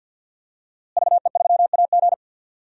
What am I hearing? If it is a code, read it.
Morse code, heard as a keyed tone, about 46 wpm, 700 Hz. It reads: VE3AG
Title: DXCC-100 Call Signs - 5 Characters - Repeated 2x - 50wpm